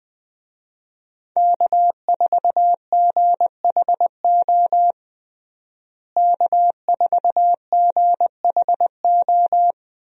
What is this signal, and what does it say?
Morse code, heard as a keyed tone, about 20 wpm, 700 Hz. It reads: K4GHO K4GHO
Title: USA Call Signs - 5 Characters - Repeated 2x - 20wpm